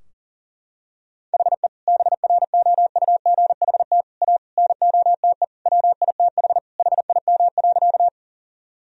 Morse code, heard as a keyed tone, about 40 wpm, 700 Hz. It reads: HE BROUGHT A NOTE WITH HIM.
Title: Sentences from Top 1000 Words - Review Part 2 - 40wpm